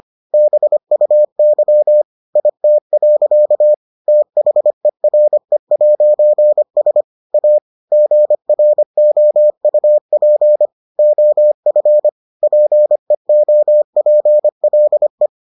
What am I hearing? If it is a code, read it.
Morse code, heard as a keyed tone, about 25 wpm, 600 Hz. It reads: BUY IT. THERE'S A GROUP OF PEOPLE